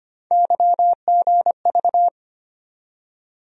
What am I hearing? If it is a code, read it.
Morse code, heard as a keyed tone, about 25 wpm, 700 Hz. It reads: YGV
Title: USA Call Sign Postfixes - 3 Letters 25wpm